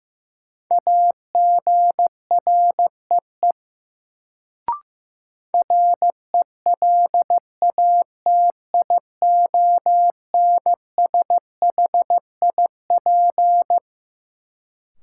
Morse code, heard as a keyed tone, about 15 wpm, 700 Hz. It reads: AGREE RELATIONSHIP
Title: Sentences from Top 1100 Words - New Word Drill Alt - 15wpm